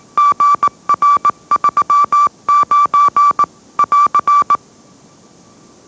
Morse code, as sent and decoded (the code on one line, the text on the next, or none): --. .-. ...-- ----. / .-.-.
GR39 <AR>